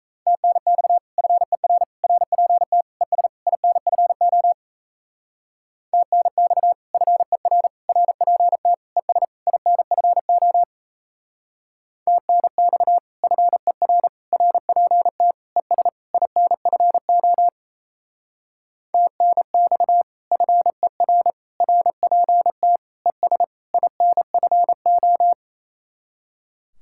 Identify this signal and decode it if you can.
Morse code, heard as a keyed tone, about 36 wpm, 700 Hz. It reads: TNX FER RPT ES INFO TNX FER RPT ES INFO TNX FER RPT ES INFO TNX FER RPT ES INFO